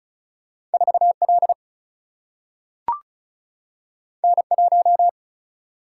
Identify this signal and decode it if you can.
Morse code, heard as a keyed tone, about 36 wpm, 700 Hz. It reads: 4 L N 1
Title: Single-Letter Single-Number 35wpm